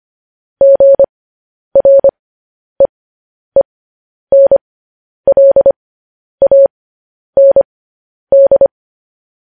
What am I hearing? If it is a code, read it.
Morse code, heard as a keyed tone, about 25 wpm, spaced as for 10 wpm, 550 Hz. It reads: GREENLAND
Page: news headlines in morse code at 10 wpm